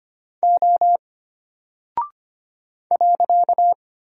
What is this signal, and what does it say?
Morse code, heard as a keyed tone, about 25 wpm, 700 Hz. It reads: O .
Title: Single Character (Letter:Number:Punctuation:Prosign) Alt 25wpm